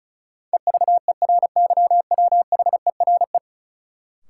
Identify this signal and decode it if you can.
Morse code, heard as a keyed tone, about 36 wpm, 700 Hz. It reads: EVERYWHERE
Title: Sentences from Top 1600 Words - New Word Drill - 35wpm